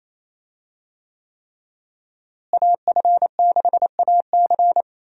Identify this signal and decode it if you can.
Morse code, heard as a keyed tone, about 28 wpm, 700 Hz. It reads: AF6AC
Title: USA Call Signs - 5 Characters - Repeated 3x - 28wpm